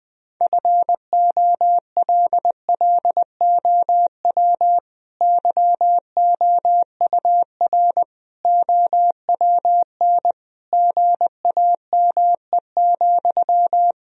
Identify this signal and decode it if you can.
Morse code, heard as a keyed tone, about 20 wpm, 700 Hz. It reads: FOLLOW YOUR OWN GAME,
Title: Sentences from Top 1000 Words - Review Part 3 - 20wpm